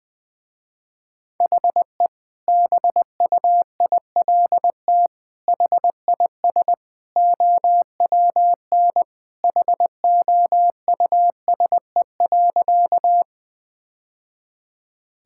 Morse code, heard as a keyed tone, about 20 wpm, 700 Hz. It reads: HE BUILT HIS OWN HOUSE.
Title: Sentences with 2-Word Phrases - Spoken Beginning - 20wpm